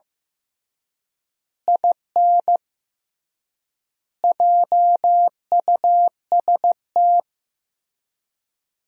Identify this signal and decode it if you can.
Morse code, heard as a keyed tone, about 15 wpm, 700 Hz. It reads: IN JUST
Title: 3x Word-Spacing - Sentences from Top 300 Words - 15wpm